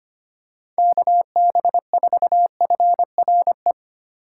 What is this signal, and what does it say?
Morse code, heard as a keyed tone, about 25 wpm, 700 Hz. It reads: KB4FRE
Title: USA Call Signs - 6 Characters - 2x Character Spacing - 25wpm